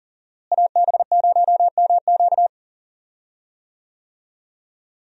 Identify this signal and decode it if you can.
Morse code, heard as a keyed tone, about 38 wpm, 700 Hz. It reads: AB0MQ
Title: USA Call Signs - 5 Characters - Repeated 2x - 40wpm